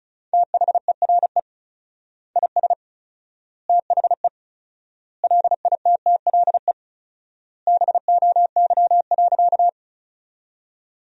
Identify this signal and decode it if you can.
Morse code, heard as a keyed tone, about 34 wpm, 700 Hz. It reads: THERE IS THE LITTLE BOY.
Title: 4x Word-Spacing - Sentences from Top 200 Words - 35wpm